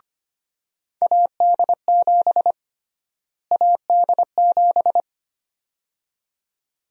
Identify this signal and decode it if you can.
Morse code, heard as a keyed tone, about 25 wpm, 700 Hz. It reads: AD7 AD7
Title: USA Call Sign Prefixes - Repeated 2x - 25wpm